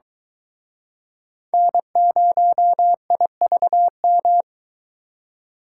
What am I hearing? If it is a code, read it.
Morse code, heard as a keyed tone, about 23 wpm, 700 Hz. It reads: N0IVM